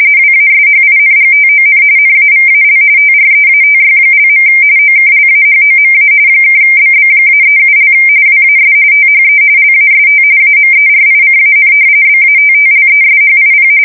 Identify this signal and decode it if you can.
RTTY, about 45.45 baud. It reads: WASHINGTON NAVAL TREATY). OKLAHOMA (BB-37) WAS THE LAST AMERICAN BATTLESHIP
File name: fsk170 / battleships